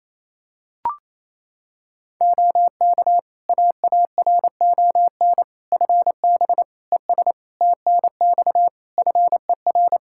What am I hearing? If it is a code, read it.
Morse code, heard as a keyed tone, about 27 wpm, 700 Hz. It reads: OK AARON FB ES TNX FER
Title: K1USN SST QSO Practice 28wpm